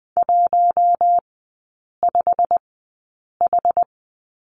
Morse code, heard as a keyed tone, about 20 wpm, spaced as for 8 wpm, 700 Hz. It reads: 15H
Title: 087 Lesson 15e - Number 5 - Three Characters 20wpm